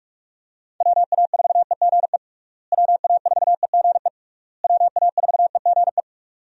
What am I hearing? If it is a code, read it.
Morse code, heard as a keyed tone, about 46 wpm, 700 Hz. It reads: WA4EGE WA4EGE WA4EGE
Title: USA Call Signs - 6 Characters - Repeated 3x - 45wpm